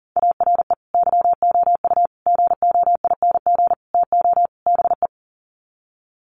Morse code, heard as a keyed tone, about 40 wpm, 700 Hz. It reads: ARE YOU GOING TO BE
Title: Sets of 5 Words Encore 40wpm